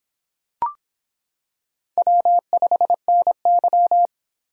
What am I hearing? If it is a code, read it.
Morse code, heard as a keyed tone, about 25 wpm, 700 Hz. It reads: W5NY